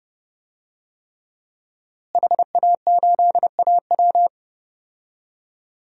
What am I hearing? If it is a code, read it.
Morse code, heard as a keyed tone, about 30 wpm, 700 Hz. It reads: HA8AW
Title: DXCC-100 Call Signs - 5 Characters - Repeated 3x - 30wpm